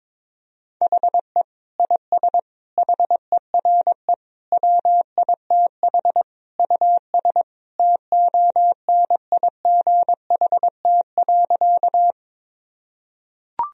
Morse code, heard as a keyed tone, about 22 wpm, 700 Hz. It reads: HE IS HERE WITH US TONIGHT.
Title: Sentences from Top 1000 Words - Review Part 3 - 22wpm